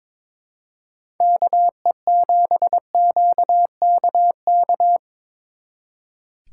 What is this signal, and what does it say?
Morse code, heard as a keyed tone, about 22 wpm, 700 Hz. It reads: KE7QKK
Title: Speed Racing - USA 2x3 Call Signs 22wpm